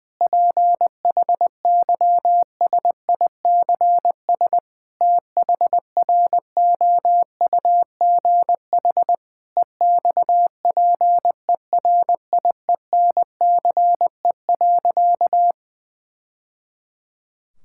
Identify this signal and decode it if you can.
Morse code, heard as a keyed tone, about 20 wpm, 700 Hz. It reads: PHYSICS THROUGH EXPERIENCE.
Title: Sentences from Top 1000 Words - Review Part 5 - 20wpm